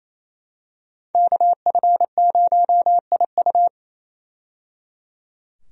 Morse code, heard as a keyed tone, about 27 wpm, 700 Hz. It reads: KF0IU
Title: USA Call Signs - 5 Characters - 28wpm